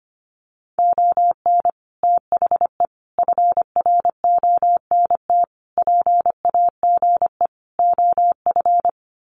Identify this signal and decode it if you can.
Morse code, heard as a keyed tone, about 25 wpm, 700 Hz. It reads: ON THE FRONT PAGE OF
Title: Sets of 5 Words Encore 25wpm